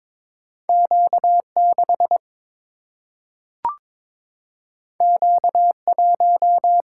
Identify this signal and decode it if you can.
Morse code, heard as a keyed tone, about 22 wpm, 700 Hz. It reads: Q6 Q1
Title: Single-Letter Single-Number 22wpm